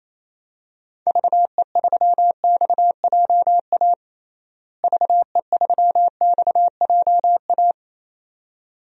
Morse code, heard as a keyed tone, about 28 wpm, 700 Hz. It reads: VE3XJA VE3XJA
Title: DXCC-100 Call Signs - 6 Characters - Repeated 2x - 28wpm